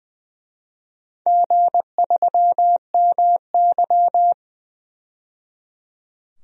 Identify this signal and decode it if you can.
Morse code, heard as a keyed tone, about 20 wpm, 700 Hz. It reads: G3MY
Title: DXCC-100 Call Signs - 4 Characters - 20wpm